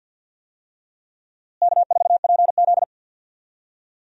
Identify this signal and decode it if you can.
Morse code, heard as a keyed tone, about 52 wpm, 700 Hz. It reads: K4PZ
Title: USA Call Signs - 4 Characters - Repeated 3x - 50wpm